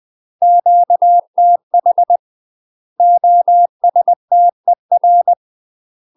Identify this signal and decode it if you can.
Morse code, heard as a keyed tone, about 20 wpm, 700 Hz. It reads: QTH OSTER
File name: UK MASTER QSOs TRAINING / 20WPM / UK MASTER QSOs 0040